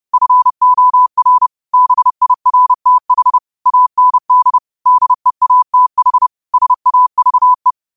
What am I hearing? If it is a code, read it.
Morse code, heard as a keyed tone, about 30 wpm, 1000 Hz. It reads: FOR BIRTH AND DEATH SAVE
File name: qotdfiles / 30wpm